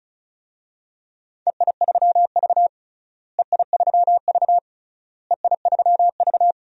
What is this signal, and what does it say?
Morse code, heard as a keyed tone, about 34 wpm, 700 Hz. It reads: EI3V EI3V EI3V
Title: DXCC-100 Call Signs - 4 Characters - Repeated 3x - 35wpm